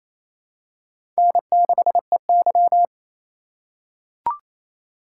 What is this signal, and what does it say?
Morse code, heard as a keyed tone, about 28 wpm, 700 Hz. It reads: N6EY
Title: USA Call Signs - 4 Characters - 2x Character Spacing - 28wpm